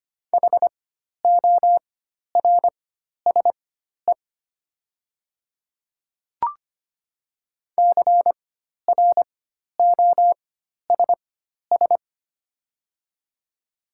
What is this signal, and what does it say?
Morse code, heard as a keyed tone, about 25 wpm, spaced as for 12 wpm, 700 Hz. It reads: HORSE CROSS
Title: 4x Character Spacing - Top 300 Words - 25wpm